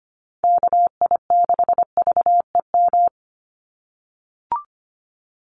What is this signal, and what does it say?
Morse code, heard as a keyed tone, about 25 wpm, 700 Hz. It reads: KI6VEM
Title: USA Call Signs 25wpm